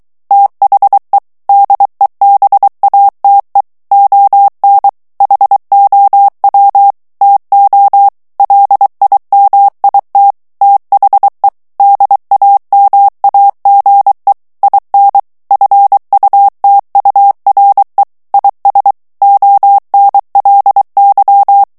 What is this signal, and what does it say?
Morse code, heard as a keyed tone, about 23 wpm, 800 Hz. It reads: THE DEBATE ON HOW TO LIMIT THE DAMAGE IN FUTURE IS ONLY
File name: morse mp3 / ANGLAIS texte clair 2 vitesse 22 mots minute